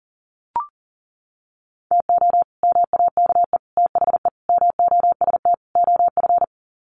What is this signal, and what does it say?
Morse code, heard as a keyed tone, about 40 wpm, 700 Hz. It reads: TO MAKE THE MOST OF
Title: Sets of 5 Words Encore2 40wpm